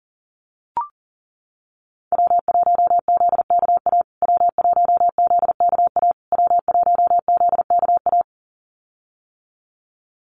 Morse code, heard as a keyed tone, about 40 wpm, 700 Hz. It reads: W1ZKA W1ZKA W1ZKA
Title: USA Call Signs Repeated 3x 40wpm